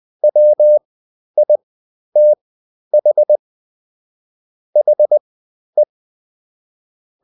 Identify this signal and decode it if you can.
Morse code, heard as a keyed tone, about 20 wpm, spaced as for 11 wpm, 600 Hz. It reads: WITH HE
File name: Top-100-Words-10